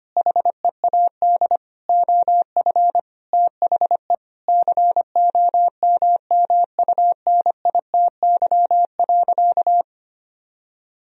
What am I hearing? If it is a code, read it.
Morse code, heard as a keyed tone, about 25 wpm, 700 Hz. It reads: HEAD OF THE COMMUNITY.
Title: Sentences from Top 1000 Words - Review Part 4 - 25wpm